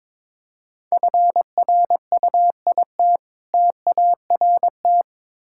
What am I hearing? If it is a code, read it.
Morse code, heard as a keyed tone, about 22 wpm, 690 Hz. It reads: FRUIT TART
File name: Difficult Words with Hints - 22wpm